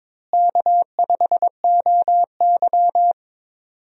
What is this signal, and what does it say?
Morse code, heard as a keyed tone, about 21 wpm, 700 Hz. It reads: K5OY
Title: USA Call Signs - 4 Characters - Repeated 2x - 22wpm